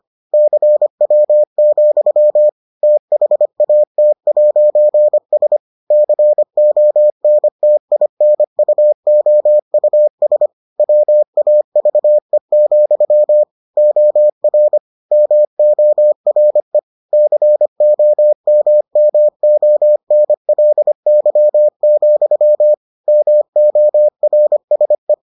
Morse code, heard as a keyed tone, about 25 wpm, 600 Hz. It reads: CW, THAT'S CONTINUOUS WAVE, OR MORE COMMONLY, MORSE